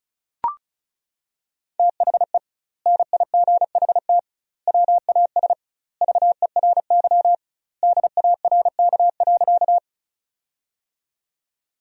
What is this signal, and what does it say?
Morse code, heard as a keyed tone, about 36 wpm, 700 Hz. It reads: THE NIGHT WAS VERY DARK.
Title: 2x Word-Spacing - Sentences from Top 500 Words - 35wpm